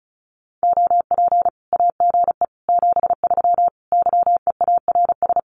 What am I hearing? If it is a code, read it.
Morse code, heard as a keyed tone, about 34 wpm, 700 Hz. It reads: OP AGE 73 YEARS
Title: Common QSO Phrases 35wpm